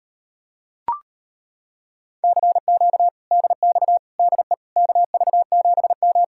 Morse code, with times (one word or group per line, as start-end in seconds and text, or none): 2.2-3.1 CQ
3.3-4.0 DX
4.2-4.5 DE
4.8-6.2 KV7M